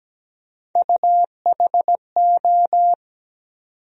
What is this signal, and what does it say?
Morse code, heard as a keyed tone, about 17 wpm, 700 Hz. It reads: UHO
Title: USA Call Sign Postfixes - 3 Letters 17wpm